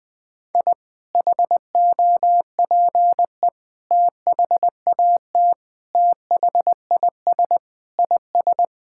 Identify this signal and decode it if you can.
Morse code, heard as a keyed tone, about 21 wpm, 700 Hz. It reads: I HOPE THAT THIS IS
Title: Sentences with 3-Word Phrases - 20wpm